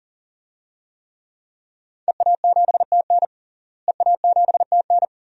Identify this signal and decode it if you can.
Morse code, heard as a keyed tone, about 40 wpm, 700 Hz. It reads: EA7TN EA7TN
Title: DXCC-100 Call Signs - 5 Characters - Repeated 2x - 40wpm